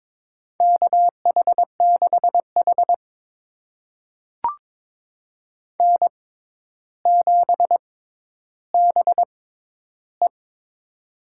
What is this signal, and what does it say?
Morse code, heard as a keyed tone, about 22 wpm, spaced as for 11 wpm, 700 Hz. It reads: KH6H N7BE
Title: USA Call Signs - 4 Characters - 6x Character Spacing - 22wpm